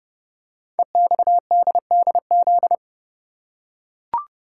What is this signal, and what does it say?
Morse code, heard as a keyed tone, about 28 wpm, 700 Hz. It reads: EXDDZ
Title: Code Groups - 5 Letters - 30wpm